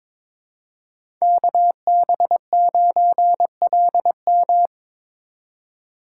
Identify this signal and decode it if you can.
Morse code, heard as a keyed tone, about 22 wpm, 700 Hz. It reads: KB9LM
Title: USA Call Signs - 5 Characters - 3x Character Spacing - 22wpm